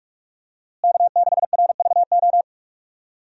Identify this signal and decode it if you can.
Morse code, heard as a keyed tone, about 46 wpm, 700 Hz. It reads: K6RVO
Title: USA Call Signs - 5 Characters - Repeated 3x - 45wpm